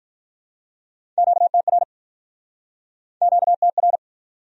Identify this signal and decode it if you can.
Morse code, heard as a keyed tone, about 51 wpm, 700 Hz. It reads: QTR QTR